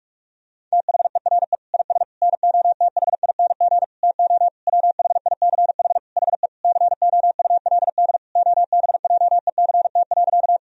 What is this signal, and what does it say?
Morse code, heard as a keyed tone, about 46 wpm, 700 Hz. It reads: THERE IS NOTHING TO WHICH HE COULD OBJECT.
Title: Sentences from Top 1000 Words - Review Part 3 - 45wpm